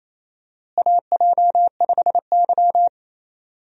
Morse code, heard as a keyed tone, about 28 wpm, 700 Hz. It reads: AJ5Y
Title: USA Call Signs - 4 Characters - Repeated 3x - 28wpm